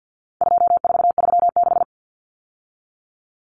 Morse code, heard as a keyed tone, about 50 wpm, 700 Hz. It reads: 2436
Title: Numbers - 4-Digit 50wpm